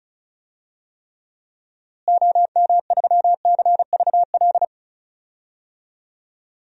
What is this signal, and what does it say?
Morse code, heard as a keyed tone, about 36 wpm, 700 Hz. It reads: OM3CVL
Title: DXCC-100 Call Signs - 6 Characters - Repeated 3x - 35wpm